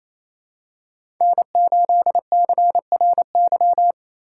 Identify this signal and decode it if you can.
Morse code, heard as a keyed tone, about 28 wpm, 700 Hz. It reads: N8CRY
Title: USA Call Signs - 5 Characters - 4x Character Spacing - 28wpm